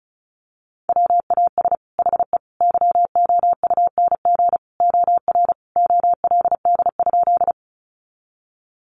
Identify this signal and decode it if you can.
Morse code, heard as a keyed tone, about 34 wpm, 700 Hz. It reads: WAS HE YOUNG OR OLD?
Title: Sentences from Top 400 Words 35wpm